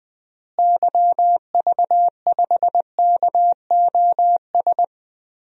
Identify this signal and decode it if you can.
Morse code, heard as a keyed tone, about 20 wpm, 700 Hz. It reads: YV5KOS
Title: DXCC-100 Call Signs - 6 Characters - Repeated 2x - 20wpm